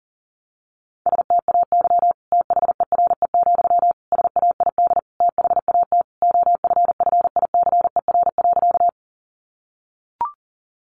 Morse code, heard as a keyed tone, about 40 wpm, 700 Hz. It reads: STAY THERE, SAID THAT OFFICER.
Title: Sentences from Top 1200 Words 40wpm